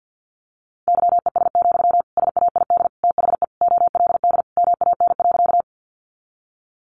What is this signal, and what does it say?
Morse code, heard as a keyed tone, about 50 wpm, 700 Hz. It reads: YES, SAID THE OLD MAN.
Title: Sentences from Top 500 Words 50wpm